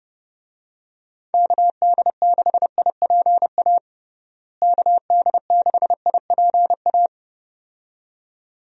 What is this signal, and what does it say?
Morse code, heard as a keyed tone, about 30 wpm, 700 Hz. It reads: KD6IPA KD6IPA
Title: USA Call Signs - 6 Characters - Repeated 2x - 30wpm